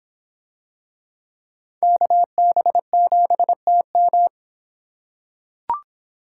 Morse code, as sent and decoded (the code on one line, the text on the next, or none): -.- -... --... - --
KB7TM